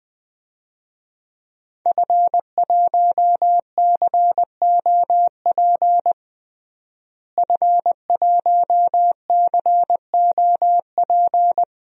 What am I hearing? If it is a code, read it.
Morse code, heard as a keyed tone, about 20 wpm, 700 Hz. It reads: F1COP F1COP
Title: DXCC-100 Call Signs - 5 Characters - Repeated 3x - 20wpm